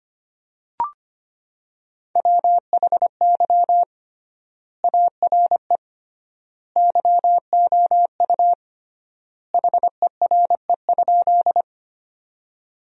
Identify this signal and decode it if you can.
Morse code, heard as a keyed tone, about 25 wpm, 700 Hz. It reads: WHY ARE YOU HERE?
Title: 3x Word-Spacing - Sentences from Top 200 Words - 25wpm